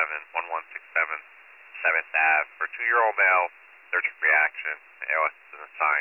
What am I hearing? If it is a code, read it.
Emergency Alert System (SAME) tones, heard as harsh buzzing bursts, with speech. It is heard from 0.7 to 4.8 s.